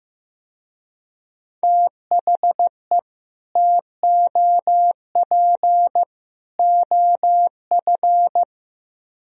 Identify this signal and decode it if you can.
Morse code, heard as a keyed tone, about 17 wpm, 700 Hz. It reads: THE TOP OF
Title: Speed-Racing - Sets of 3 Words 15wpm